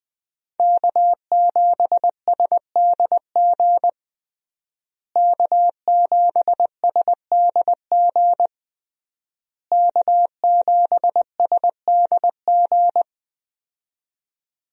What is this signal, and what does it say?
Morse code, heard as a keyed tone, about 20 wpm, 700 Hz. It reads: K7SDG K7SDG K7SDG
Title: USA Call Signs - 5 Characters - Repeated 3x - 20wpm